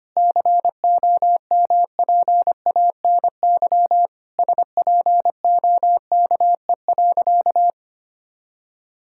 Morse code, heard as a keyed tone, about 25 wpm, 700 Hz. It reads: COMPANY SPOKE.
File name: practice / 25wpm / Sentences with 3-Word Phrases - 25wpm